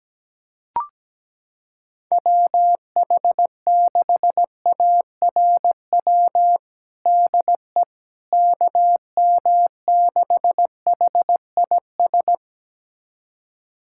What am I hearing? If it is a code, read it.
Morse code, heard as a keyed tone, about 17 wpm, 700 Hz. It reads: WH6ARW DE KM6HIS
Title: USA Callsign DE Callsign - 6 Characters - 17wpm